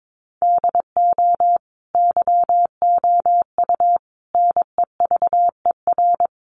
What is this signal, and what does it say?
Morse code, heard as a keyed tone, about 22 wpm, 700 Hz. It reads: DO YOU NEVER
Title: Sentences from Top 1500 Words 22wpm